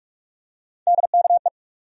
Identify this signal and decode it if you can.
Morse code, heard as a keyed tone, about 41 wpm, 700 Hz. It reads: DKE
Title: USA Call Sign Postfixes - 3 Letters 45wpm